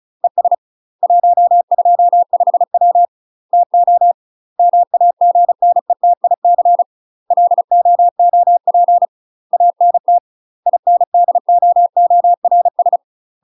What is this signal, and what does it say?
Morse code, heard as a keyed tone, about 34 wpm, 700 Hz. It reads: ES 125W TO MAGNETIC LOOP ANT INDOORS
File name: UK MASTER QSOs TRAINING / 35WPM / UK MASTER QSOs 0000